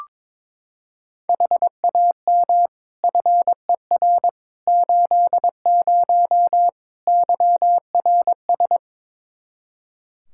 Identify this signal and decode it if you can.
Morse code, heard as a keyed tone, about 22 wpm, 700 Hz. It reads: HAM FER 80 YRS
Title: Practice QSO - Exchange 3 - More Info - 22wpm